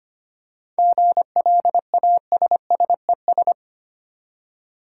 Morse code, heard as a keyed tone, about 25 wpm, 690 Hz. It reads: GLASSES
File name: practice / 25wpm / Easy Words with Hints - 25wpm